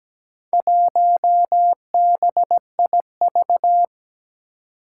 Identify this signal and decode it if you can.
Morse code, heard as a keyed tone, about 17 wpm, 700 Hz. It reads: 1BIV
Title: Four Challenging Letter-Number Combinations 17wpm